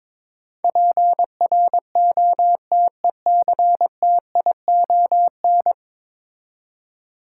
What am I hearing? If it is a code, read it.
Morse code, heard as a keyed tone, about 22 wpm, 700 Hz. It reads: PROTECTION
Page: morse-code-ninja-practice-22wpm